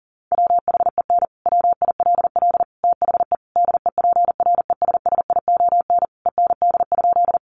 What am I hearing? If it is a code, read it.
Morse code, heard as a keyed tone, about 40 wpm, 700 Hz. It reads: WHEN WILL THE DEPRESSION END?
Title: Sentences from Top 1300 Words 40wpm